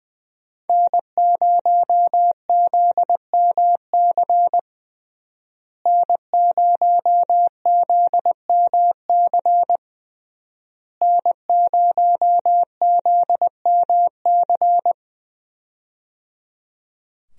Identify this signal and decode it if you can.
Morse code, heard as a keyed tone, about 20 wpm, 700 Hz. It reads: N0ZMC N0ZMC N0ZMC
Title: USA Call Signs - 5 Characters - Repeated 3x - 20wpm